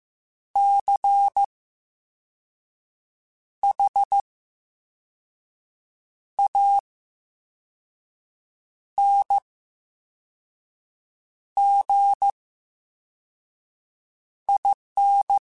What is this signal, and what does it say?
Morse code, heard as a keyed tone, about 15 wpm, spaced as for 4 wpm, 800 Hz. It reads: CHANGIN
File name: Special INGs / Special INGs-mp3s / Slow ing @13